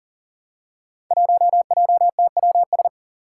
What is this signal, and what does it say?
Morse code, heard as a keyed tone, about 42 wpm, 700 Hz. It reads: 1JTWS
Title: Five Challenging Letter-Number Combinations 40wpm